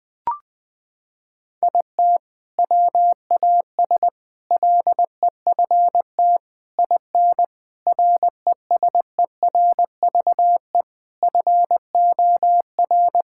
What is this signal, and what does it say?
Morse code, heard as a keyed tone, about 21 wpm, 700 Hz. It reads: IT WAS LEFT IN RESERVE FOR
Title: Sentences from Top 1000 Words - Review Part 6 - 20wpm